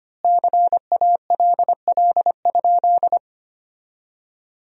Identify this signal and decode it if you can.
Morse code, heard as a keyed tone, about 25 wpm, 700 Hz. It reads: CALL?
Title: 3x Word-Spacing - Sentences from Top 100 Words - 25wpm